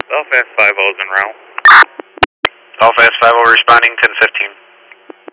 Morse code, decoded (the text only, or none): none